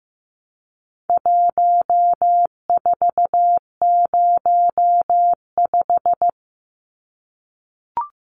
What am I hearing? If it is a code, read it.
Morse code, heard as a keyed tone, about 15 wpm, 700 Hz. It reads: 1405
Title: Numbers - 4-Digit 15wpm